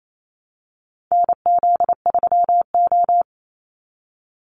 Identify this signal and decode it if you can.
Morse code, heard as a keyed tone, about 28 wpm, 700 Hz. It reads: NZ3O
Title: Call Signs - 28wpm